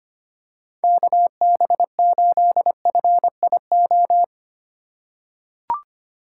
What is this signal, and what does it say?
Morse code, heard as a keyed tone, about 25 wpm, 700 Hz. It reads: KB8FIO
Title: USA Call Signs - 6 Characters - 3x Character Spacing - 25wpm